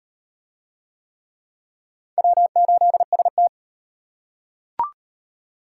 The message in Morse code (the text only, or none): W8ST